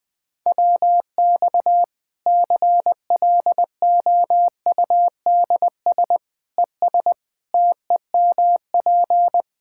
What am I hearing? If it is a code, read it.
Morse code, heard as a keyed tone, about 20 wpm, 700 Hz. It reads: WX CLOUDS ES TEMP
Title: K1USN SST QSO Practice 20wpm